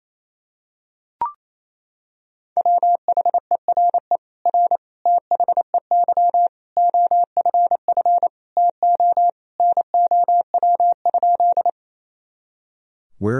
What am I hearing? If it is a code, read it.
Morse code, heard as a keyed tone, about 28 wpm, 700 Hz. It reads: WHERE R THEY OFF TO NOW?
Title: Sentences from Top 200 Words and Abbreviations 28wpm